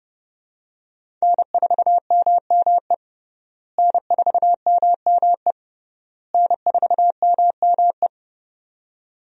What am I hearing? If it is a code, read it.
Morse code, heard as a keyed tone, about 32 wpm, 700 Hz. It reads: N4MME N4MME N4MME
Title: USA Call Signs - 5 Characters - Repeated 3x - 30wpm